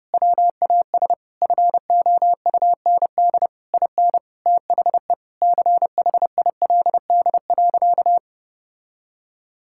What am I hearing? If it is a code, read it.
Morse code, heard as a keyed tone, about 30 wpm, 700 Hz. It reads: WAS FOUND IN THE CHILD.
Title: Sentences from Top 1000 Words - Review Part 5 - 30wpm